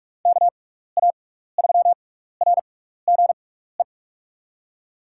Morse code, heard as a keyed tone, about 42 wpm, spaced as for 15 wpm, 700 Hz. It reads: KA3RGE
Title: USA Call Signs - 6x Character Spacing - 45wpm